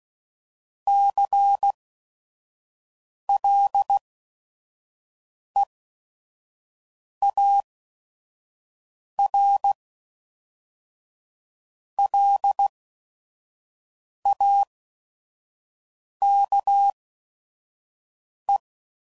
Morse code, heard as a keyed tone, about 16 wpm, spaced as for 5 wpm, 800 Hz. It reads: CLEAR LAKE
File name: CW Code Groups @5WPM / fromSM's / Cities